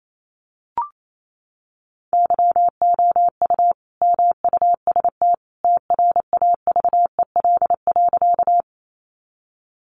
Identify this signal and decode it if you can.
Morse code, heard as a keyed tone, about 28 wpm, 700 Hz. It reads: YOU MUST TRAVEL.
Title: Sentences from Top 400 Words 28wpm